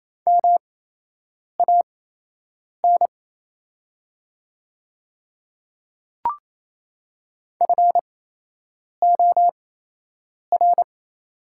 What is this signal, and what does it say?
Morse code, heard as a keyed tone, about 28 wpm, spaced as for 8 wpm, 700 Hz. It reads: MAN FOR
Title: 8x Character Spacing - Top 200 Words - 28wpm